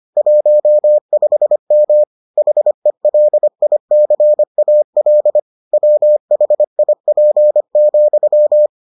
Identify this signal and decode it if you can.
Morse code, heard as a keyed tone, about 25 wpm, 600 Hz. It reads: 15M HELICAL WHIP,